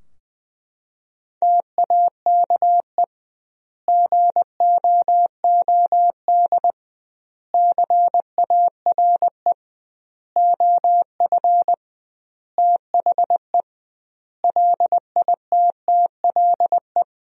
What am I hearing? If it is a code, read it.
Morse code, heard as a keyed tone, about 20 wpm, 700 Hz. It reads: TAKE GOOD CARE OF THE LITTLE
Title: 2x Word-Spacing - Sentences from Top 400 Words - 20wpm